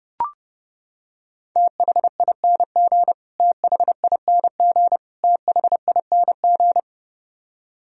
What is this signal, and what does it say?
Morse code, heard as a keyed tone, about 32 wpm, 700 Hz. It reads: THING THING THING
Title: Top 100 Words Repeated 30wpm